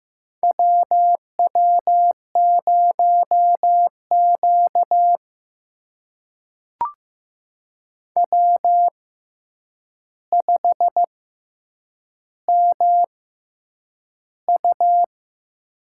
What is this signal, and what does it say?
Morse code, heard as a keyed tone, about 15 wpm, spaced as for 8 wpm, 700 Hz. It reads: WW0Q W5MU
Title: USA Call Signs - 4 Characters - 6x Character Spacing - 15wpm